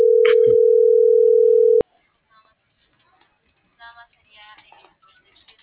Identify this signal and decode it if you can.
Ringback tone (North America), 440 + 480 Hz.